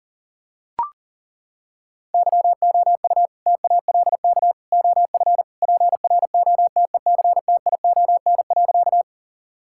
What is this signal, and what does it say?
Morse code, heard as a keyed tone, about 40 wpm, 700 Hz. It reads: YOU TALK OF PROTECTION.
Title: Sentences from Top 1600 Words 40wpm